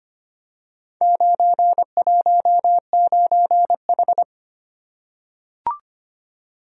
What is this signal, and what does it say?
Morse code, heard as a keyed tone, about 25 wpm, 700 Hz. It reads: 919H
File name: Four Letter-Number Combinations - 25wpm